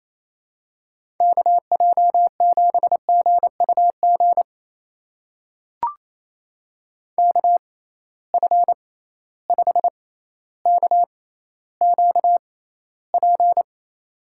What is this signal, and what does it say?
Morse code, heard as a keyed tone, about 28 wpm, spaced as for 14 wpm, 700 Hz. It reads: KJ7GUG KF5KQP
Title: USA Call Signs - 6 Characters - 6x Character Spacing - 28wpm